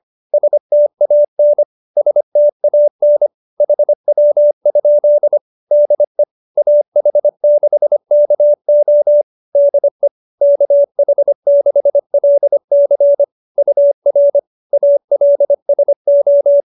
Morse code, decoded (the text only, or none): STAN STAN HW? DE AH6KO DE KH6LC UR ALSO